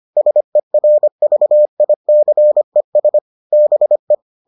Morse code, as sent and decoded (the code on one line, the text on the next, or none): ... .-. ...- .. -.-. ... / -... .
SERVICES BE